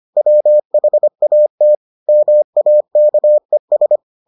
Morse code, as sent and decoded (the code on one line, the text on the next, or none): .-- .... .- - / -- .- -.- . ...
WHAT MAKES